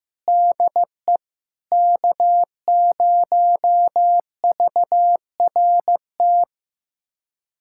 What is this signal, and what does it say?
Morse code, heard as a keyed tone, about 15 wpm, 700 Hz. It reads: DE K0VRT
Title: USA Callsign DE Callsign - 5 Characters - 15wpm